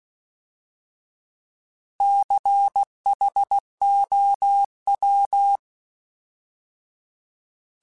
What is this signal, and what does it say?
Morse code, heard as a keyed tone, about 16 wpm, 800 Hz. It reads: CHOW
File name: Alphabetical / Alphabetical mp3s / Cs @15